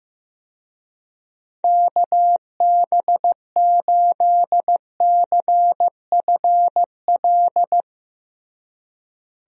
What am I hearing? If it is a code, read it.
Morse code, heard as a keyed tone, about 15 wpm, 700 Hz. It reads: KB8CFL
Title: USA Call Signs - 6 Characters - Repeated 2x - 15wpm